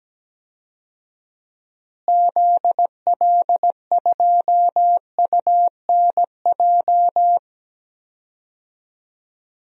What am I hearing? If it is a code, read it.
Morse code, heard as a keyed tone, about 17 wpm, 700 Hz. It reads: ZL2UNJ
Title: DXCC-100 Call Signs - 6 Characters - Repeated 3x - 17wpm